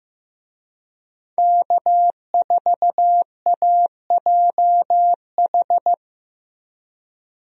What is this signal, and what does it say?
Morse code, heard as a keyed tone, about 15 wpm, 700 Hz. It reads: K4AJH
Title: USA Call Signs - 5 Characters - Repeated 2x - 15wpm